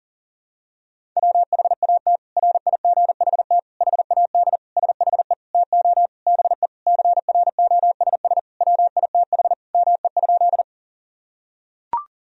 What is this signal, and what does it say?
Morse code, heard as a keyed tone, about 40 wpm, 700 Hz. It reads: WHAT RIGHT HAD SHE TO BE CROSS WITH ME?
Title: Sentences from Top 1000 Words - Review Part 1 - 40wpm